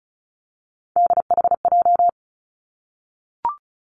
Morse code, heard as a keyed tone, about 36 wpm, 700 Hz. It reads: DHJ
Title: Three Challenging Letter-Number Combinations 35wpm